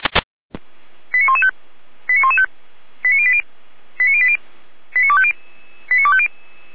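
Five-tone selective call (ZVEI), heard as quick runs of five tones, tones of about 70 ms. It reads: 89115 89115 89080 89080 89250 89250
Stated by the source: ZVEI five-tone selective call 89315 received by SDR, from the Feuerwehr Orlamünde (fire brigade) audio server.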